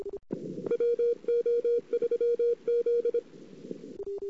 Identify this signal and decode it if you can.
Morse code, heard as a keyed tone, about 26 wpm, 470 Hz. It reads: WO3Z
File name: CW / 2008-ss-cw-k5tr-op-n5rz -22:00